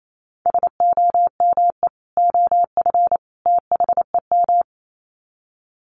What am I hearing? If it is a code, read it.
Morse code, heard as a keyed tone, about 28 wpm, 700 Hz. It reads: SOME OF THEM
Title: Sets of 3 Words 28wpm